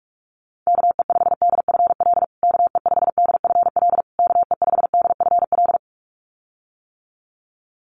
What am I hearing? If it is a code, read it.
Morse code, heard as a keyed tone, about 44 wpm, 700 Hz. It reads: KE5DFL KE5DFL KE5DFL
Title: USA Call Signs Repeated 3x 45wpm